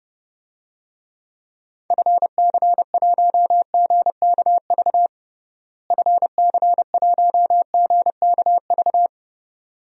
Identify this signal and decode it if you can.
Morse code, heard as a keyed tone, about 29 wpm, 700 Hz. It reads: FC1GKV FC1GKV
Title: DXCC-100 Call Signs - 6 Characters - Repeated 2x - 30wpm